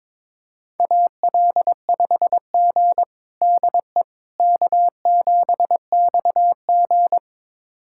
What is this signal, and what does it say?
Morse code, heard as a keyed tone, about 22 wpm, 700 Hz. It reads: AL5G DE K7XG
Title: USA Callsign DE Callsign - 4 Characters - 22wpm